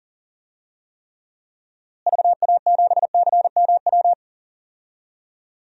Morse code, heard as a keyed tone, about 37 wpm, 700 Hz. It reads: VA7CMW
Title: DXCC-100 Call Signs - 6 Characters - Repeated 3x - 40wpm